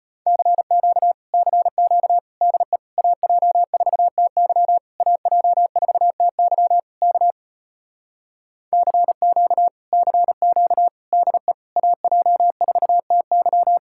Morse code, heard as a keyed tone, about 33 wpm, 700 Hz. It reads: CQ CQ DE AJ4TY AJ4TY K CQ CQ DE AJ4TY